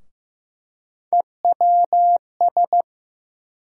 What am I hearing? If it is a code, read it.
Morse code, heard as a keyed tone, about 15 wpm, 700 Hz. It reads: EWS
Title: USA Call Sign Postfixes - 3 Letters 15wpm